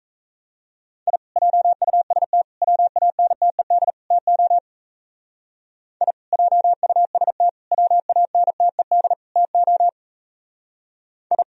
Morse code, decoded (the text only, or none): I JUST WANTED TO I JUST WANTED TO I